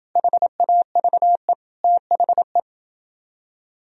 Morse code, heard as a keyed tone, about 28 wpm, 700 Hz. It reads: HAVE THE